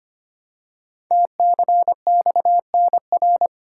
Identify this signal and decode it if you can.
Morse code, heard as a keyed tone, about 25 wpm, 700 Hz. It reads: TCXNR
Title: Code Groups - 5 Letters - 25wpm